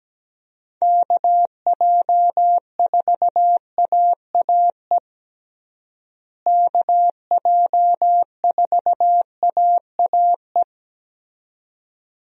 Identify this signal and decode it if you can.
Morse code, heard as a keyed tone, about 17 wpm, 700 Hz. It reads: KJ4AAE KJ4AAE
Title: USA Call Signs - 6 Characters - Repeated 2x - 17wpm